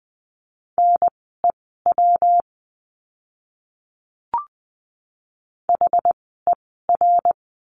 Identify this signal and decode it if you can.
Morse code, heard as a keyed tone, about 20 wpm, spaced as for 15 wpm, 700 Hz. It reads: NEW HER
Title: 160 Lesson 26f - Letter g - Words 20wpm